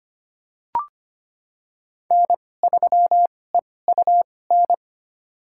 Morse code, heard as a keyed tone, about 25 wpm, spaced as for 18 wpm, 700 Hz. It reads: N3EUN
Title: USA Call Signs - 5 Characters - 2x Character Spacing - 25wpm